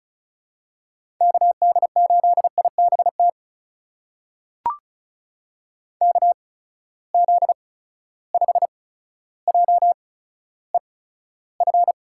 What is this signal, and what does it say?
Morse code, heard as a keyed tone, about 36 wpm, spaced as for 15 wpm, 700 Hz. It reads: KD8IBT KZ5JEF
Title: USA Call Signs - 6 Characters - 8x Character Spacing - 35wpm